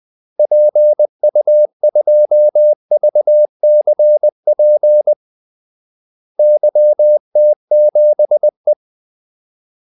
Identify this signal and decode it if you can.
Morse code, heard as a keyed tone, about 20 wpm, 600 Hz. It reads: PU2VCP YT7E
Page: Kaldesignaler 20 wpm